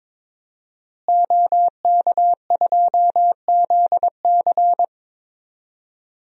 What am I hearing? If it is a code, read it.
Morse code, heard as a keyed tone, about 22 wpm, 700 Hz. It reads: OK2ZC
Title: DXCC-100 Call Signs - 5 Characters - Repeated 2x - 22wpm